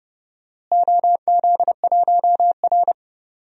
Morse code, heard as a keyed tone, about 30 wpm, 700 Hz. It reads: OZ1R
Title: DXCC-100 Call Signs - 4 Characters - Repeated 3x - 30wpm